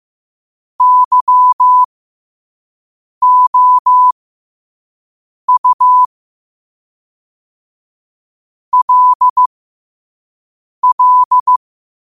Morse code, heard as a keyed tone, about 15 wpm, spaced as for 5 wpm, 1000 Hz. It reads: YOU LL
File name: qotdfiles / 5wpm